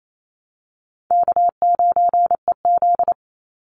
Morse code, heard as a keyed tone, about 28 wpm, 700 Hz. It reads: K9EZ
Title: NA SOTA Chaser Callsigns 28wpm